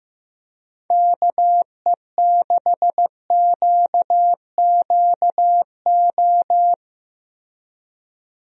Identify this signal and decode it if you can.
Morse code, heard as a keyed tone, about 15 wpm, 700 Hz. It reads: KE6QQO
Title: USA Call Signs - 6 Characters - Repeated 3x - 15wpm